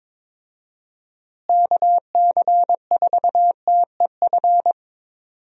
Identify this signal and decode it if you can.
Morse code, heard as a keyed tone, about 22 wpm, 700 Hz. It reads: KC4TEF
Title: USA Call Signs - 6 Characters - Repeated 2x - 22wpm